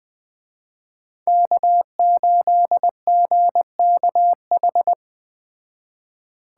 Morse code, heard as a keyed tone, about 20 wpm, 700 Hz. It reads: K8GKH
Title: Speed Racing - USA 1x3 Call Signs 17wpm